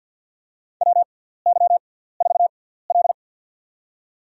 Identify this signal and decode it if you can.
Morse code, heard as a keyed tone, about 52 wpm, spaced as for 17 wpm, 700 Hz. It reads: WY4L
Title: USA Call Signs - 6x Character Spacing - 50wpm